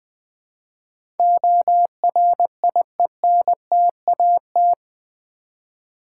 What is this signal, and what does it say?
Morse code, heard as a keyed tone, about 20 wpm, 700 Hz. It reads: ORIENTAT
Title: Build A Word - 4 Letters At A Time - 11-Letter Words - 20wpm